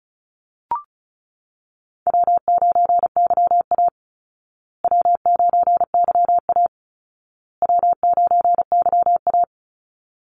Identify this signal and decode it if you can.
Morse code, heard as a keyed tone, about 34 wpm, 700 Hz. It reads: W9YA W9YA W9YA
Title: Easy Call Signs Repeated 3x - 35wpm